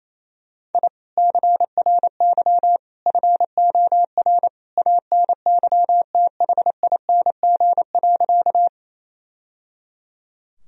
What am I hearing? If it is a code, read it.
Morse code, heard as a keyed tone, about 27 wpm, 700 Hz. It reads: I CRY FOR ANYTHING.
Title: Sentences from Top 1000 Words - Review Part 2 - 28wpm